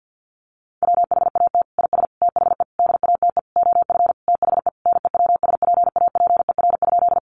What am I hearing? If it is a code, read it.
Morse code, heard as a keyed tone, about 50 wpm, 700 Hz. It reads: WHAT IS THE DATE OF THE NEWSPAPER?
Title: Sentences from Top 1100 Words 50wpm